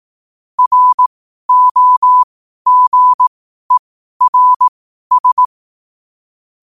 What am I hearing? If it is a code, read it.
Morse code, heard as a keyed tone, about 18 wpm, spaced as for 12 wpm, 1000 Hz. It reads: ROGERS
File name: qotdfiles / 12wpm